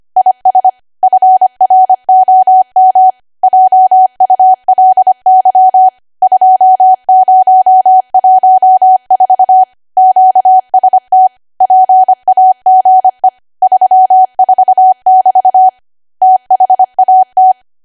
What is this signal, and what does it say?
Morse code, heard as a keyed tone, about 25 wpm, 750 Hz. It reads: IS FROM JULY 2014 QST PAGE 34<BT> THAT
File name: Archive / 25 WPM / 25WPM